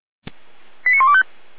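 Five-tone selective call (ZVEI), heard as quick runs of five tones, tones of about 70 ms. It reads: 89125
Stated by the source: ZVEI five-tone call received on an SDR.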